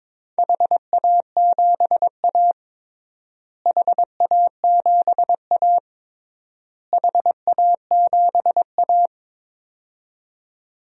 Morse code, heard as a keyed tone, about 22 wpm, 700 Hz. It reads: HA7A HA7A HA7A
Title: DXCC-100 Call Signs - 4 Characters - Repeated 3x - 22wpm